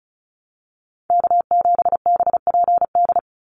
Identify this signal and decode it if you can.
Morse code, heard as a keyed tone, about 36 wpm, 700 Hz. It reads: K7BPD
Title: USA Call Signs 35wpm